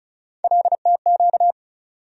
Morse code, heard as a keyed tone, about 36 wpm, 700 Hz. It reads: LTQ
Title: USA Call Sign Postfixes - 3 Letters 35wpm